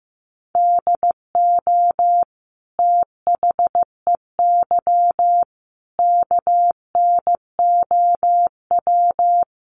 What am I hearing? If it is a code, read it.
Morse code, heard as a keyed tone, about 15 wpm, 700 Hz. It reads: DO THEY KNOW
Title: Sentences from Top 100 Words 15wpm